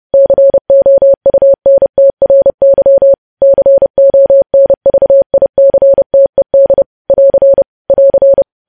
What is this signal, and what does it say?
Morse code, heard as a keyed tone, about 30 wpm, 550 Hz. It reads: COUNTRY CONVICTED <AR> <AR>
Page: news headlines in morse code at 30 wpm